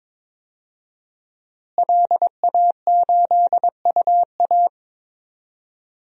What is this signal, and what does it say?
Morse code, heard as a keyed tone, about 22 wpm, 700 Hz. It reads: LA8UA
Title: DXCC-100 Call Signs - 5 Characters - Repeated 3x - 22wpm